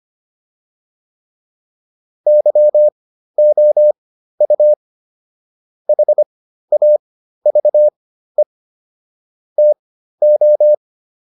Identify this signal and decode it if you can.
Morse code, heard as a keyed tone, about 25 wpm, spaced as for 13 wpm, 600 Hz. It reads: YOU HAVE TO